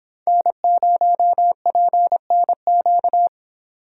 Morse code, heard as a keyed tone, about 26 wpm, 700 Hz. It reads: N0PNQ